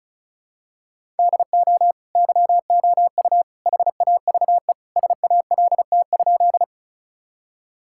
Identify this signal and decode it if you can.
Morse code, heard as a keyed tone, about 34 wpm, 700 Hz. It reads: DO YOU HAVE SALT?
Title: Sentences from Top 1000 Words - Review Part 5 - 35wpm